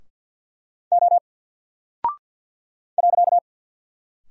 Morse code, heard as a keyed tone, about 49 wpm, 700 Hz. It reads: O .
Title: Single Character (Letter:Number:Punctuation:Prosign) Alt 50wpm